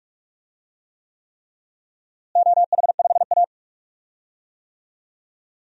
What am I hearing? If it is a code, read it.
Morse code, heard as a keyed tone, about 45 wpm, 700 Hz. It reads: OH5A
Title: DXCC-100 Call Signs - 4 Characters - Repeated 3x - 45wpm